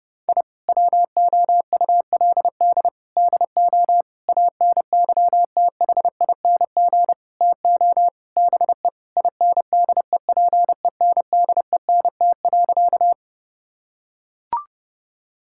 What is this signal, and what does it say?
Morse code, heard as a keyed tone, about 30 wpm, 700 Hz. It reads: I WOULD DO ANYTHING TO BE INDEPENDENT.
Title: Sentences from Top 1000 Words - Review Part 6 - 30wpm